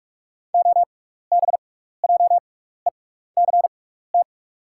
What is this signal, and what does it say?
Morse code, heard as a keyed tone, about 43 wpm, spaced as for 15 wpm, 700 Hz. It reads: OBJECT
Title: 6x Character Spacing - Top 500 Words - 45wpm